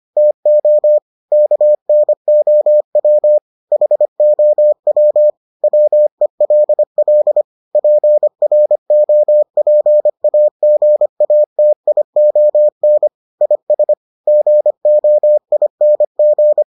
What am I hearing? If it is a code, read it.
Morse code, heard as a keyed tone, about 25 wpm, 600 Hz. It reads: TO KNOW HOW WELL PROPAGATION IS GOING